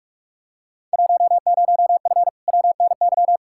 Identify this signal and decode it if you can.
Morse code, heard as a keyed tone, about 46 wpm, 700 Hz. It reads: 10F WNY
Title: Field Day Exchanges Encore - 45wpm